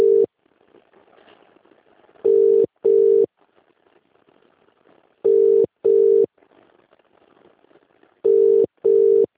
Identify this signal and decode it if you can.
Ringback tone (UK), 400 + 450 Hz.